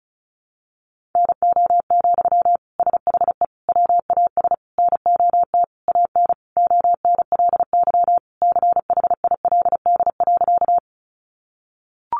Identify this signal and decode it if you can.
Morse code, heard as a keyed tone, about 34 wpm, 700 Hz. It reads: NO, SHE WAS NOT AN ONLY CHILD.
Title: Sentences from Top 800 Words 35wpm